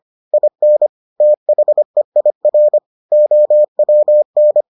Morse code, heard as a keyed tone, about 25 wpm, 600 Hz. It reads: IN THEIR OWN